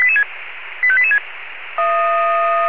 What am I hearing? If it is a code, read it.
Five-tone selective call (ZVEI), heard as quick runs of five tones, tones of about 70 ms. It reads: 85886